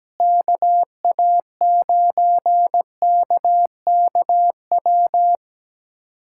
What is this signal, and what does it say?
Morse code, heard as a keyed tone, about 17 wpm, 700 Hz. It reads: KA9KKW
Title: USA Callsign DE Callsign 17wpm